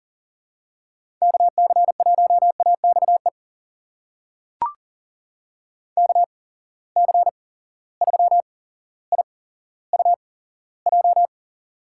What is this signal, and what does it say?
Morse code, heard as a keyed tone, about 40 wpm, spaced as for 17 wpm, 700 Hz. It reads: KC1AXE KC3IUJ
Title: USA Call Signs - 6 Characters - 8x Character Spacing - 40wpm